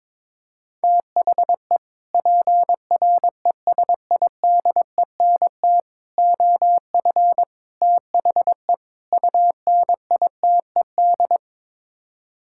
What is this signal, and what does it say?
Morse code, heard as a keyed tone, about 22 wpm, 700 Hz. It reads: THE PRESIDENT OF THE UNITED
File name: practice / 22wpm / Build A Phrase - 5-Word Phrases - 22wpm